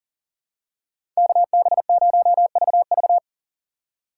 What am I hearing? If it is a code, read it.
Morse code, heard as a keyed tone, about 40 wpm, 700 Hz. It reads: KB0VV
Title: USA Call Signs - 5 Characters - 3x Character Spacing - 40wpm